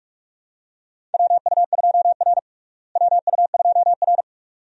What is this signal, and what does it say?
Morse code, heard as a keyed tone, about 42 wpm, 700 Hz. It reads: WU2R WU2R
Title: USA Call Signs - 4 Characters - Repeated 2x - 45wpm